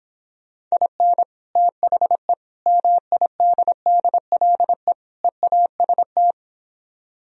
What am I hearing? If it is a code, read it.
Morse code, heard as a keyed tone, about 26 wpm, 700 Hz. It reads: IN THE MIDDLE EAST